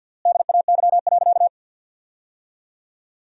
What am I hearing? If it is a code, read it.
Morse code, heard as a keyed tone, about 50 wpm, 700 Hz. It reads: DAY.
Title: 3x Word-Spacing - Sentences from Top 200 Words - 50wpm